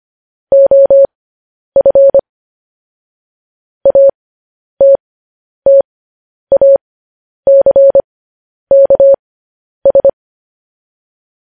Morse code, heard as a keyed tone, about 25 wpm, spaced as for 10 wpm, 550 Hz. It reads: OF ATTACKS
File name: podcast / News headlines-1x-25x10